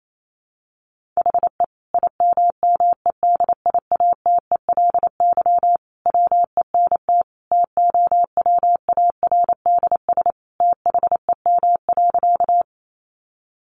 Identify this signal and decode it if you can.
Morse code, heard as a keyed tone, about 28 wpm, 700 Hz. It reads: HE IMMEDIATELY WENT TOWARDS THEM.
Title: Sentences from Top 1100 Words 28wpm